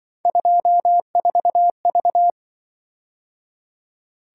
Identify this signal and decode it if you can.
Morse code, heard as a keyed tone, about 24 wpm, 700 Hz. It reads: 24V
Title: Speed-Racing - Three Letter-Number Combinations - 20wpm